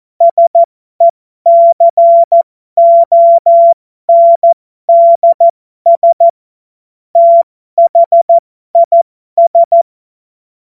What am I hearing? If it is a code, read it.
Morse code, heard as a keyed tone, about 14 wpm, spaced as for 12 wpm, 680 Hz. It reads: SECONDS THIS